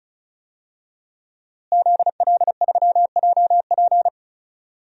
Morse code, heard as a keyed tone, about 36 wpm, 700 Hz. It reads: ZL3JP
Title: DXCC-100 Call Signs - 5 Characters - Repeated 2x - 35wpm